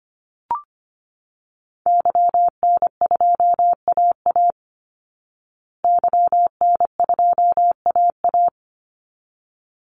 Morse code, heard as a keyed tone, about 25 wpm, 700 Hz. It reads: YN2AA YN2AA
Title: Call Signs Repeated 3x - 25wpm